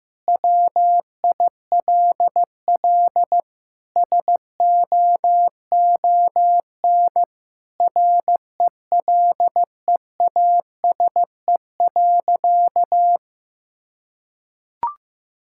Morse code, heard as a keyed tone, about 15 wpm, 700 Hz. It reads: WILL SOON RELEASE.
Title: Sentences from Top 1600 Words 15wpm